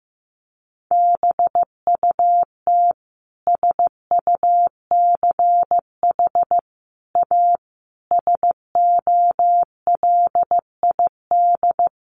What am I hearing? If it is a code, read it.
Morse code, heard as a keyed tone, about 15 wpm, 700 Hz. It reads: BUT SUCH A SOLID
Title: Sentences from Top 1500 Words 15wpm